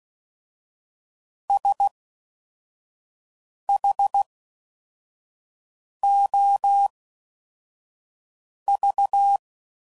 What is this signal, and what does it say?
Morse code, heard as a keyed tone, about 16 wpm, spaced as for 4 wpm, 800 Hz. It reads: SHOV